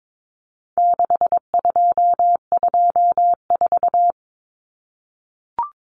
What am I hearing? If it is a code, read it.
Morse code, heard as a keyed tone, about 22 wpm, 700 Hz. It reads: 6224
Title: Common Cut Numbers - 4-Digit 22wpm